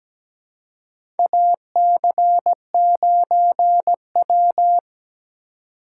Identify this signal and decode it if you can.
Morse code, heard as a keyed tone, about 17 wpm, 700 Hz. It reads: AC9W
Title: USA Call Signs - 3x Character Spacing - 17wpm